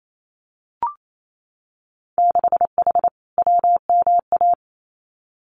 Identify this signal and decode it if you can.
Morse code, heard as a keyed tone, about 28 wpm, 700 Hz. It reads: 6H WMA
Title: Winter Field Day Exchanges Encore 28wpm